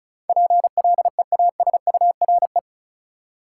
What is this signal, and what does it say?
Morse code, heard as a keyed tone, about 36 wpm, 700 Hz. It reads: PLEASURE
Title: Sentences from Top 1600 Words - New Word Drill - 35wpm